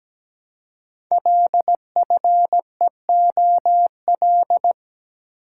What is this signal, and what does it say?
Morse code, heard as a keyed tone, about 17 wpm, 700 Hz. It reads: LFEOL
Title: Code Groups - 5 Letters - 17wpm